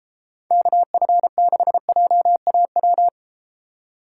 Morse code, heard as a keyed tone, about 33 wpm, 700 Hz. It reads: KF6JAW